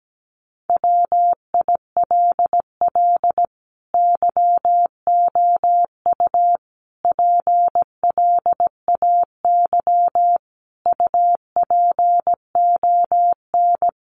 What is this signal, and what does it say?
Morse code, heard as a keyed tone, about 17 wpm, 700 Hz. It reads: WILL YOU PLAY UPON
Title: Sentences from Top 1500 Words 17wpm